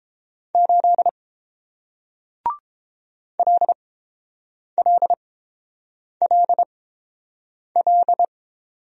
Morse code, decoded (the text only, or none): 8 LLLL